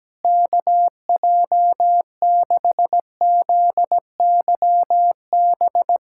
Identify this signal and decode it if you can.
Morse code, heard as a keyed tone, about 17 wpm, 700 Hz. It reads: KJ6ZYB
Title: USA Callsign DE Callsign - 6 Characters - 17wpm